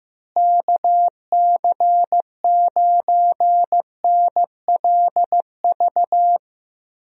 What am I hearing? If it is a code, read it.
Morse code, heard as a keyed tone, about 15 wpm, 700 Hz. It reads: KC9NLV
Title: USA Call Signs - 6 Characters - Repeated 3x - 15wpm